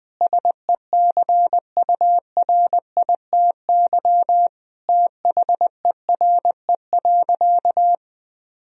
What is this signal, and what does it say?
Morse code, heard as a keyed tone, about 21 wpm, 700 Hz. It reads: SECURITY THERE.
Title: Sentences from Top 1000 Words - Review Part 4 - 20wpm